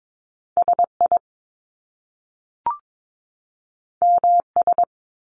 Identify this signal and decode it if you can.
Morse code, heard as a keyed tone, about 22 wpm, 700 Hz. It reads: SI MS
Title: Top Two Letter Combinations in Words 22wpm